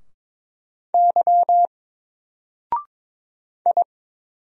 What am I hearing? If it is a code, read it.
Morse code, heard as a keyed tone, about 22 wpm, 700 Hz. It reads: Y I
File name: Single-Letter-Number-22wpm